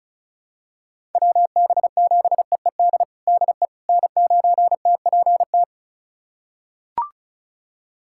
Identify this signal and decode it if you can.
Morse code, heard as a keyed tone, about 36 wpm, 700 Hz. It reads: WB7EED DE N9TPT
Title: USA Callsign DE Callsign 35wpm